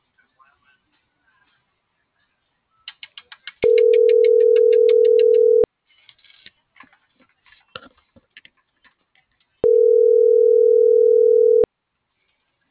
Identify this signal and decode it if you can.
Ringback tone (North America), 440 + 480 Hz.